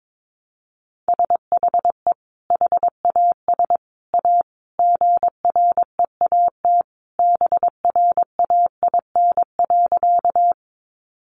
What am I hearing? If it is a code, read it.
Morse code, heard as a keyed tone, about 22 wpm, 700 Hz. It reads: SHE HAS A GREAT BRAIN.
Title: Sentences from Top 1100 Words 22wpm